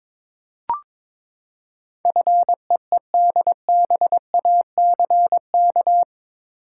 Morse code, heard as a keyed tone, about 21 wpm, 700 Hz. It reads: FEEDBACK
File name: practice / 22wpm / Eight Letter Words - 22wpm